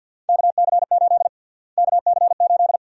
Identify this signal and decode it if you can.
Morse code, heard as a keyed tone, about 52 wpm, 700 Hz. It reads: KC8 KC8
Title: USA Call Sign Prefixes - Repeated 2x - 50wpm